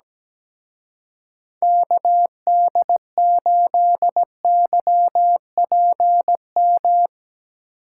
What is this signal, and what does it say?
Morse code, heard as a keyed tone, about 20 wpm, 700 Hz. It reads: KD8YPM
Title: Speed Racing - USA 2x3 Call Signs 17wpm